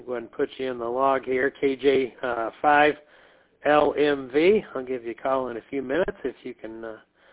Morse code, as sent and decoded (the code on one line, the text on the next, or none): none